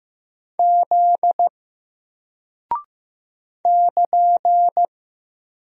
Morse code, heard as a keyed tone, about 15 wpm, 700 Hz. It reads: Z <KN>
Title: Single Character (Letter:Number:Punctuation:Prosign) Alt 15wpm